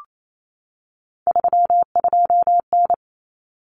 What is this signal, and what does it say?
Morse code, heard as a keyed tone, about 28 wpm, 700 Hz. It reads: 32N
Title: Common Cut Numbers - 3-Digit 28wpm